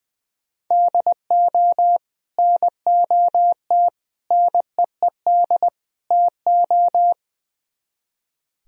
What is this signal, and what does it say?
Morse code, heard as a keyed tone, about 20 wpm, 700 Hz. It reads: DO NOT NEED TO
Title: Speed-Racing - Sets of 4 Words 20wpm